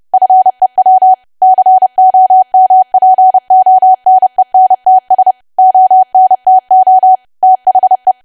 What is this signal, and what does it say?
Morse code, heard as a keyed tone, about 30 wpm, 750 Hz. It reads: FEW COMPONENTS ONTO THE
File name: cwa / W1AW-Level3 / 30WPM